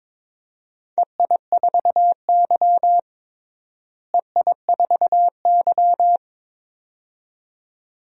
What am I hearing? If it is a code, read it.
Morse code, heard as a keyed tone, about 21 wpm, 700 Hz. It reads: EI4Y EI4Y
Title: DXCC-100 Call Signs - 4 Characters - Repeated 2x - 22wpm